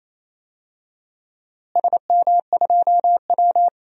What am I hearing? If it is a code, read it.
Morse code, heard as a keyed tone, about 28 wpm, 700 Hz. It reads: SM2W
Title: DXCC-100 Call Signs - 4 Characters - Repeated 3x - 28wpm